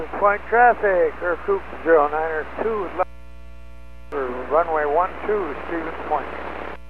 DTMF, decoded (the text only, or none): none